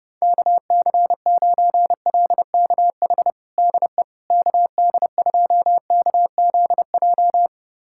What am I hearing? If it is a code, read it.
Morse code, heard as a keyed tone, about 30 wpm, 700 Hz. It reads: KC9LKH DE KD2KZJ
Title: USA Callsign DE Callsign 30wpm